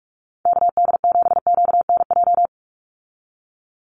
Morse code, heard as a keyed tone, about 44 wpm, 700 Hz. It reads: KD7QNJ
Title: USA Call Signs 45wpm